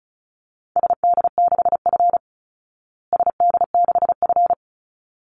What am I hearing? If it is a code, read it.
Morse code, heard as a keyed tone, about 34 wpm, 700 Hz. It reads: SD6F SD6F
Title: Call Signs Repeated 3x - 35wpm